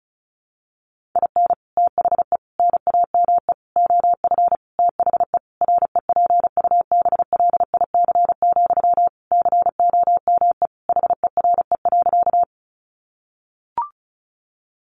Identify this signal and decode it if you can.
Morse code, heard as a keyed tone, about 34 wpm, 700 Hz. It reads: IN THE NAME OF THE REPUBLIC, COME HERE.
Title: Sentences from Top 1500 Words 35wpm